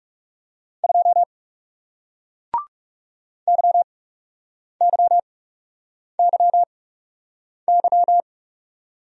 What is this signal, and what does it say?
Morse code, heard as a keyed tone, about 38 wpm, spaced as for 8 wpm, 700 Hz. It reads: 2 YYYY